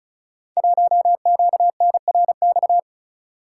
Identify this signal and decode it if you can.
Morse code, heard as a keyed tone, about 36 wpm, 700 Hz. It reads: 1QNRX
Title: Code Groups - 5 Letter:Numbers - 35wpm